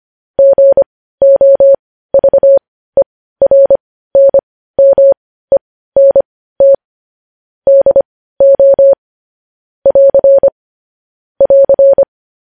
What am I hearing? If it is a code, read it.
Morse code, heard as a keyed tone, about 25 wpm, spaced as for 15 wpm, 550 Hz. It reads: GOVERNMENT DO <AR> <AR>
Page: news-headlines-in-morse-code-at-15-wpm